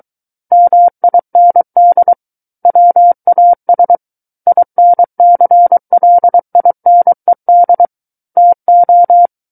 Morse code, heard as a keyed tone, about 23 wpm, 700 Hz. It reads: MIND WAS INCLINED TO
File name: Book One / 14-Chapter 14 - 23 WPM